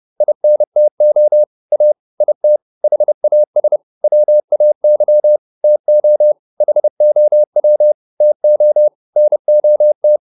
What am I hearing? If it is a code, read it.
Morse code, heard as a keyed tone, about 30 wpm, 600 Hz. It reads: INTO A IT HAS WAY TO HOW TO NOT
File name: WD104 30